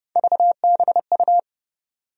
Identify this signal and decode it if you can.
Morse code, heard as a keyed tone, about 30 wpm, 700 Hz. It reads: VBU
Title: USA Call Sign Postfixes - 3 Letters 30wpm